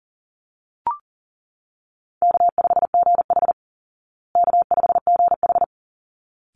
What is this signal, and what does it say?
Morse code, heard as a keyed tone, about 40 wpm, 700 Hz. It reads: K5GH K5GH
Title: Call Signs Repeated 3x - 40wpm